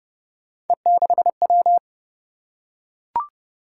Morse code, heard as a keyed tone, about 28 wpm, 700 Hz. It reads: E6W